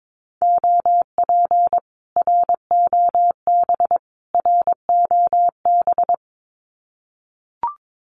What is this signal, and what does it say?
Morse code, heard as a keyed tone, about 22 wpm, 700 Hz. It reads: OP ROB ROB
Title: Common QSO Phrases Encore 22wpm